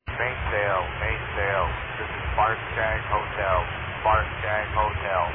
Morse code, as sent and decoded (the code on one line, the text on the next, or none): none